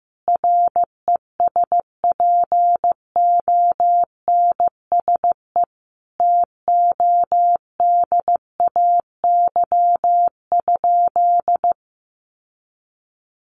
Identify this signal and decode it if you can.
Morse code, heard as a keyed tone, about 15 wpm, 700 Hz. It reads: RESPONSE TODAY?
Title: Sentences from Top 1400 Words 15wpm